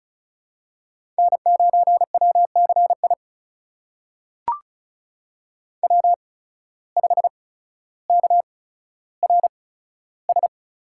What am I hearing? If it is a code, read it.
Morse code, heard as a keyed tone, about 36 wpm, spaced as for 15 wpm, 700 Hz. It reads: N9WCI W5KRS
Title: USA Call Signs - 5 Characters - 8x Character Spacing - 35wpm